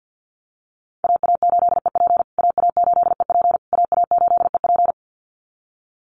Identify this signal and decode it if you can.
Morse code, heard as a keyed tone, about 49 wpm, 700 Hz. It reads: AA8EP AA8EP AA8EP
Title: USA Call Signs Repeated 3x 50wpm